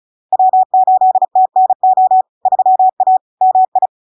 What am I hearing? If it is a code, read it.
Morse code, heard as a keyed tone, about 35 wpm, 750 Hz. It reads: W8TNO 3A MI
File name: LARK-FD-2022-Log-in-CW-35wpm-750-Hz